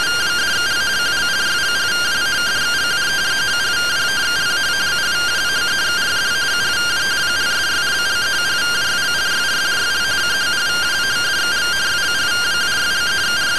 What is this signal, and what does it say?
Two-tone teleprinter signal, about 170 Hz shift.